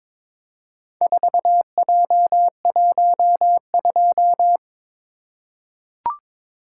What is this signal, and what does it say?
Morse code, heard as a keyed tone, about 21 wpm, 700 Hz. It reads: 4J12
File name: practice / 22wpm / Four Letter-Number Combinations - 22wpm